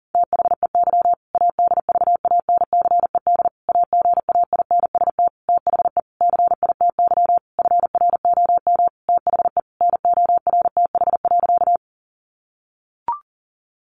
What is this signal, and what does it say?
Morse code, heard as a keyed tone, about 40 wpm, 700 Hz. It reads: THEY ADVANCED AGAINST THE CITY FROM THE NORTH.
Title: Sentences from Top 1200 Words 40wpm